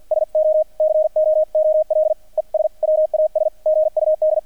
Morse code, heard as a keyed tone, about 50 wpm, 620 Hz, with radio noise. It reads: SO COOL ES WAS GUD
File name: archived / CRSnet